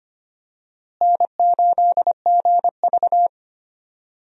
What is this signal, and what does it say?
Morse code, heard as a keyed tone, about 25 wpm, 700 Hz. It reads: N8GV
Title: USA Call Signs - 4 Characters - Repeated 2x - 25wpm